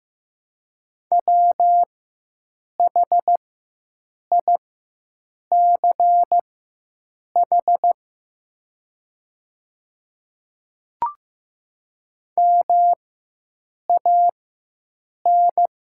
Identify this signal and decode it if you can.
Morse code, heard as a keyed tone, about 15 wpm, spaced as for 7 wpm, 700 Hz. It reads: WHICH MAN
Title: 4x Character Spacing - Top 500 Words - 15wpm